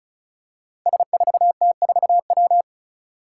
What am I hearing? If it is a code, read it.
Morse code, heard as a keyed tone, about 36 wpm, 700 Hz. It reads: S4T4W
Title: Five Challenging Letter-Number Combinations 35wpm